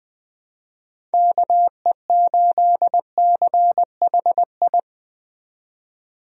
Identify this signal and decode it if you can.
Morse code, heard as a keyed tone, about 21 wpm, 700 Hz. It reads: KE8CHI
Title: USA Call Signs - 6 Characters - 20wpm